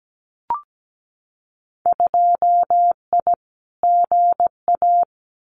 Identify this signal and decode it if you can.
Morse code, heard as a keyed tone, about 17 wpm, 700 Hz. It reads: 2I GA
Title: Winter Field Day Exchanges 17wpm